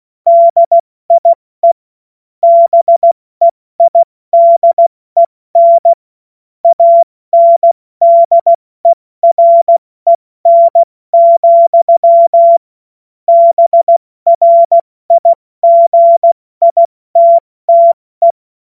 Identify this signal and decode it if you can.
Morse code, heard as a keyed tone, about 16 wpm, 680 Hz. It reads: DIE BEIDEN ANDEREN, BRIGITTE
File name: Morsekurs / Witze / Witz 04. Januar 2022, 70 F80